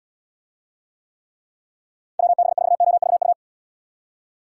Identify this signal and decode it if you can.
Morse code, heard as a keyed tone, about 76 wpm, 700 Hz. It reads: WB4OVU